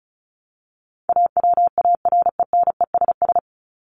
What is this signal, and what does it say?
Morse code, heard as a keyed tone, about 34 wpm, 700 Hz. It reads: AWARENESS
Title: Top 1501-1750 Words 35wpm